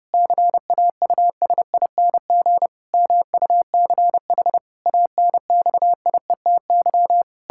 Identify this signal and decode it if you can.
Morse code, heard as a keyed tone, about 32 wpm, 700 Hz. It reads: CAUSING MUCH ANXIETY
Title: Pangram Challenge v2 - 30wpm